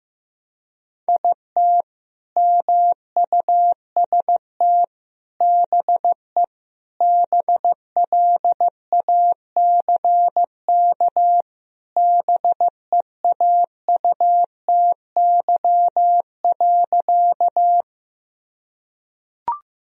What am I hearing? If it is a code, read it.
Morse code, heard as a keyed tone, about 15 wpm, 700 Hz. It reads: IT MUST BE BLACK BEAUTY.
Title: Sentences from Top 1000 Words - Review Part 2 - 15wpm